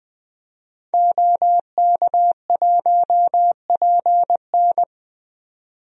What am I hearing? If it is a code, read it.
Morse code, heard as a keyed tone, about 20 wpm, 700 Hz. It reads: OK1PN
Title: DXCC-100 Call Signs - 5 Characters - 20wpm